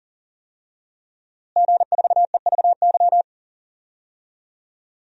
Morse code, heard as a keyed tone, about 40 wpm, 700 Hz. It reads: G4EVY
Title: DXCC-100 Call Signs - 5 Characters - Repeated 3x - 40wpm